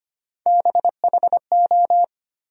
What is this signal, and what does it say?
Morse code, heard as a keyed tone, about 25 wpm, 700 Hz. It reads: BHO
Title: USA Call Sign Postfixes - 3 Letters 25wpm